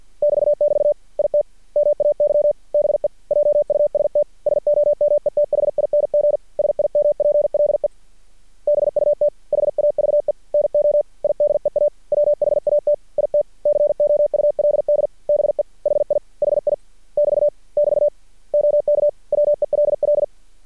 Morse code, heard as a keyed tone, about 50 wpm, 590 Hz. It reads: <BT><BT> IT MAY BE JUST SOMETHING SIMPLE BUT HAVE NO IDEA WHAT IT COULD BE HI HI <BT> <BT> OK WELL
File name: archived / CRSnet